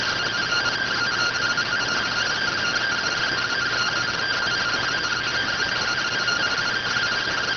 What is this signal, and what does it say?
Two-tone teleprinter signal, about 170 Hz shift.